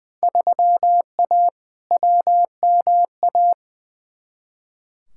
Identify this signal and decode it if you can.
Morse code, heard as a keyed tone, about 20 wpm, 700 Hz. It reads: 3A WMA
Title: Field Day Exchanges Encore - 20wpm